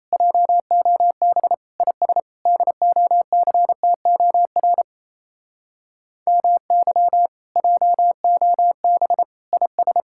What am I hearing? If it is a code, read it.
Morse code, heard as a keyed tone, about 36 wpm, 700 Hz. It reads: JOB IS DOCTOR MY JOB IS